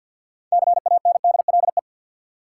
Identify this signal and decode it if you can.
Morse code, heard as a keyed tone, about 48 wpm, 700 Hz. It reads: CANDLE
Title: Six Letter Words - 50wpm